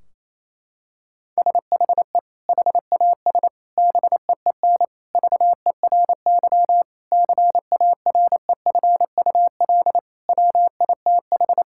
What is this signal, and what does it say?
Morse code, heard as a keyed tone, about 27 wpm, 700 Hz. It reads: SHE HAS BEEN VERY CAREFUL WITH
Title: Sentences from Top 1600 Words 28wpm